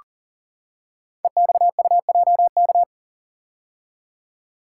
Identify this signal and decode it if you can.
Morse code, heard as a keyed tone, about 40 wpm, 700 Hz. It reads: EXUJK
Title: Code Groups - 5 Letters - 40wpm